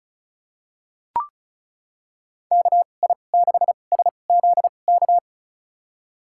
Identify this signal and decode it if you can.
Morse code, heard as a keyed tone, about 36 wpm, spaced as for 25 wpm, 700 Hz. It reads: KI6SZK
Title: USA Call Signs - 6 Characters - 2x Character Spacing - 35wpm